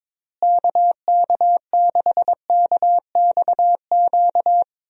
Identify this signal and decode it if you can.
Morse code, heard as a keyed tone, about 22 wpm, 700 Hz. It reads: KK6KXQ
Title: USA Call Signs - 6 Characters - Repeated 2x - 22wpm